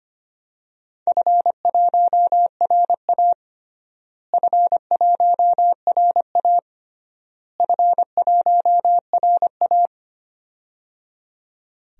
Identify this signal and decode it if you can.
Morse code, heard as a keyed tone, about 25 wpm, 700 Hz. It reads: F1RA F1RA F1RA
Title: DXCC-100 Call Signs - 4 Characters - Repeated 3x - 25wpm